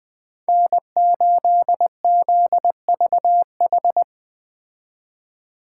Morse code, heard as a keyed tone, about 21 wpm, 700 Hz. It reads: N8ZVH
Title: Speed Racing - USA 1x3 Call Signs 17wpm